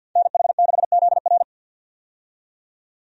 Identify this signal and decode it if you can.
Morse code, heard as a keyed tone, about 52 wpm, 700 Hz. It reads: NH6ZR
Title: USA Call Signs - 5 Characters - 50wpm